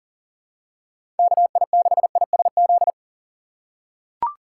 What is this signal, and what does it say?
Morse code, heard as a keyed tone, about 37 wpm, 700 Hz. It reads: KI6ISZ
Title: USA Call Signs - 6 Characters - 4x Character Spacing - 40wpm